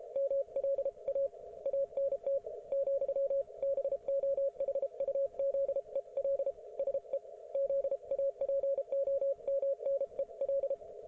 Morse code, heard as a keyed tone, about 33 wpm, spaced as for 29 wpm, 550 Hz, with radio noise. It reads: MLA ANT, BOHUZEL SE ZAPOMNEL